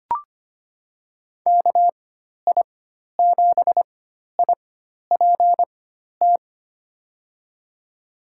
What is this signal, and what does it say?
Morse code, heard as a keyed tone, about 25 wpm, spaced as for 12 wpm, 700 Hz. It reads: KI7IPT
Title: USA Call Signs - 6 Characters - 4x Character Spacing - 25wpm